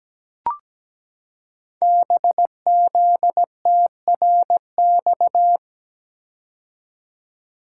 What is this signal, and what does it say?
Morse code, heard as a keyed tone, about 17 wpm, 700 Hz. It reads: BZTRX
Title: Code Groups - 5 Letter:Numbers - 17wpm